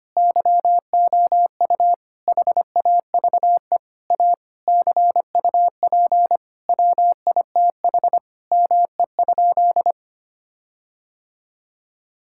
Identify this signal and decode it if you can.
Morse code, heard as a keyed tone, about 25 wpm, 700 Hz. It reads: YOU HAVE A CUP WITH ME?
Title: Sentences from Top 1600 Words 25wpm